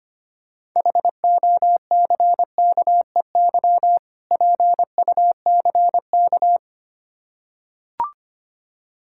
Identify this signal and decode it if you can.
Morse code, heard as a keyed tone, about 25 wpm, 700 Hz. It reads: HOCKEY PUCK
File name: practice / 25wpm / Difficult Words with Hints - 25wpm